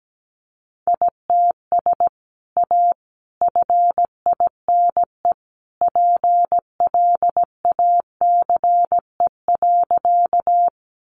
Morse code, heard as a keyed tone, about 17 wpm, 700 Hz. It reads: ITS A FINE PLACE.
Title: Sentences from Top 600 Words 17wpm